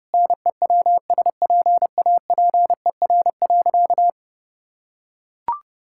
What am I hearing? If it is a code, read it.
Morse code, heard as a keyed tone, about 30 wpm, 700 Hz. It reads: NEWSPAPER.
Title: Sentences from Top 1000 Words - Review Part 6 - 30wpm